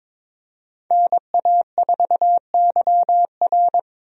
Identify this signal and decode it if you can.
Morse code, heard as a keyed tone, about 22 wpm, 700 Hz. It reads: NA4YR
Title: USA Call Signs - 5 Characters - Repeated 3x - 22wpm